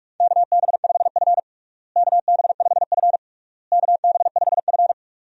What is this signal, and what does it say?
Morse code, heard as a keyed tone, about 46 wpm, 700 Hz. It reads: KB5F KB5F KB5F
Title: USA Call Signs - 4 Characters - Repeated 3x - 45wpm